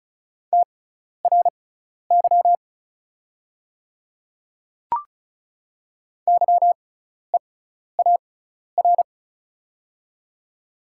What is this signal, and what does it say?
Morse code, heard as a keyed tone, about 36 wpm, spaced as for 12 wpm, 700 Hz. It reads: TRY YEAR
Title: 6x Character Spacing - Top 200 Words - 35wpm